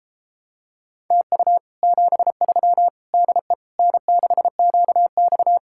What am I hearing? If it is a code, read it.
Morse code, heard as a keyed tone, about 33 wpm, 700 Hz. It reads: TU 73 DE N6QX